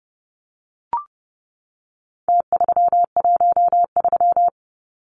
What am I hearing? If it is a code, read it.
Morse code, heard as a keyed tone, about 30 wpm, 700 Hz. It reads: T313
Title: Numbers - Contest Style - 4-Digit 30wpm